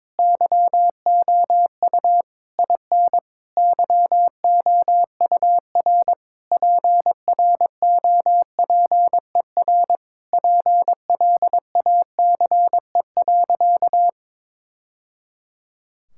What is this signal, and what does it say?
Morse code, heard as a keyed tone, about 22 wpm, 700 Hz. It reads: YOU IN YOUR PROPER PLACE.
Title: Sentences from Top 1000 Words - Review Part 5 - 22wpm